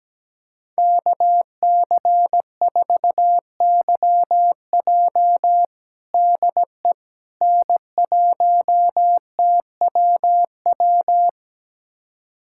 Morse code, heard as a keyed tone, about 17 wpm, 700 Hz. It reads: KC4YJ DE N1TWW
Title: USA Callsign DE Callsign - 5 Characters - 17wpm